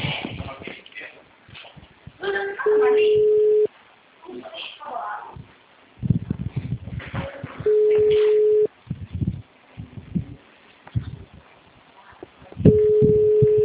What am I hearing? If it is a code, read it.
Ringback tone (Europe, 425 Hz), with speech.